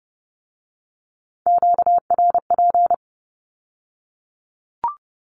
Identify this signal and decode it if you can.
Morse code, heard as a keyed tone, about 30 wpm, 700 Hz. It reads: QRP
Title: Q-Codes with Spoken Meaning - 30wpm